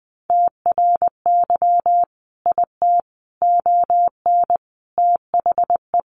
Morse code, heard as a keyed tone, about 20 wpm, 700 Hz. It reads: TRY IT ON THE
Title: Sentences from Top 400 Words 20wpm